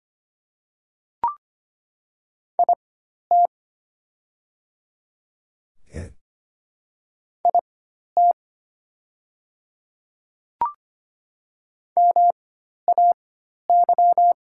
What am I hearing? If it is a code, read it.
Morse code, heard as a keyed tone, about 25 wpm, spaced as for 12 wpm, 690 Hz. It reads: IT IT MAY
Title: 4x Character Spacing - Top 100 Words - 25wpm